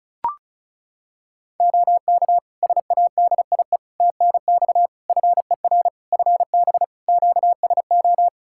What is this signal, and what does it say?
Morse code, heard as a keyed tone, about 34 wpm, 700 Hz. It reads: OK SADIE TNX FER FB QSO
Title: Practice QSO - Exchange 4 - Ending - 35wpm